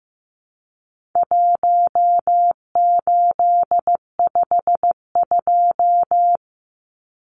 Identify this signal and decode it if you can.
Morse code, heard as a keyed tone, about 15 wpm, 700 Hz. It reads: 1852
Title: Numbers - 4-Digit 15wpm